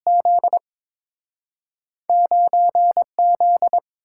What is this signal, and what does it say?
Morse code, heard as a keyed tone, about 31 wpm, spaced as for 27 wpm, 700 Hz. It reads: Z 9Z
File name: Speed-Racing - Two Letter-Number Combinations - 22wpm